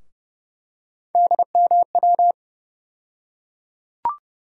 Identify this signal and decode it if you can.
Morse code, heard as a keyed tone, about 30 wpm, 700 Hz. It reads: DMW
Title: USA Call Sign Postfixes - 3 Letters 30wpm